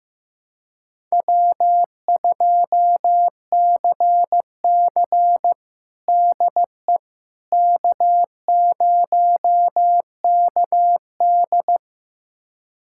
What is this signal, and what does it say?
Morse code, heard as a keyed tone, about 15 wpm, 700 Hz. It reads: W2CC DE K0KD
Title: USA Callsign DE Callsign - 4 Characters - 15wpm